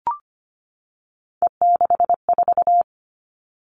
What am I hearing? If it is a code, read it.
Morse code, heard as a keyed tone, about 25 wpm, 700 Hz. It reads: E64
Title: Three Challenging Letter-Number Combinations 25wpm